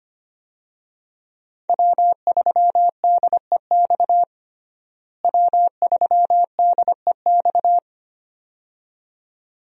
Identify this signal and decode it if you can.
Morse code, heard as a keyed tone, about 25 wpm, 700 Hz. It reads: W3DEX W3DEX
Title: USA Call Signs - 5 Characters - Repeated 2x - 25wpm